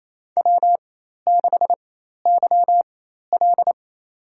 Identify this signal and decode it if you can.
Morse code, heard as a keyed tone, about 28 wpm, spaced as for 13 wpm, 700 Hz. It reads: W6YL
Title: USA Call Signs - 4x Character Spacing - 28wpm